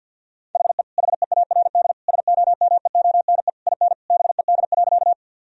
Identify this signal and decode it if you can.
Morse code, heard as a keyed tone, about 50 wpm, 700 Hz. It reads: HE HEARD SOMEONE IN BED.
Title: Sentences from Top 1000 Words - Review Part 4 - 50wpm